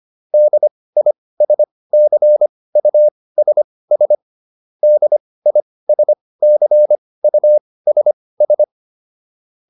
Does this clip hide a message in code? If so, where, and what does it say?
Yes: Morse code from 0.3 to 8.6 s, heard as a keyed tone, about 25 wpm, spaced as for 18 wpm, 600 Hz. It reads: DISCUSS DISCUSS